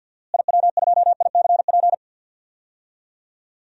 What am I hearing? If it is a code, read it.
Morse code, heard as a keyed tone, about 51 wpm, 700 Hz. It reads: IW2ICP
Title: DXCC-100 Call Signs - 6 Characters - Repeated 2x - 50wpm